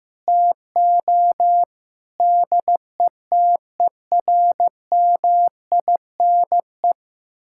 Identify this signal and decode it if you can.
Morse code, heard as a keyed tone, about 15 wpm, 700 Hz. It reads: TO DETERMINE
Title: Sentences from Top 1000 Words - Review Part 5 - 15wpm